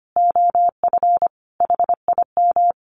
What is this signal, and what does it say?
Morse code, heard as a keyed tone, about 25 wpm, 700 Hz. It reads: OF HIM
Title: Sentences from Top 1300 Words 25wpm